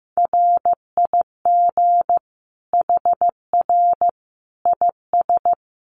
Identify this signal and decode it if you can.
Morse code, heard as a keyed tone, about 15 wpm, 700 Hz. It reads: RIG HR IS
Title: Common QSO Phrases 15wpm